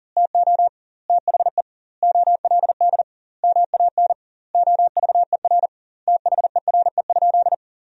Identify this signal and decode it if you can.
Morse code, heard as a keyed tone, about 40 wpm, 700 Hz. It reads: TO THE OLD MAN OVER THERE?
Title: 2x Word-Spacing - Sentences from Top 200 Words - 40wpm